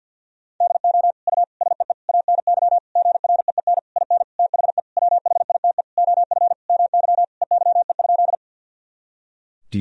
Morse code, heard as a keyed tone, about 53 wpm, 700 Hz. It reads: DO U SEE ANY GREEN IN THE WHITE OF MY EYE?
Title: Sentences from Top 500 Words and Abbreviations 50wpm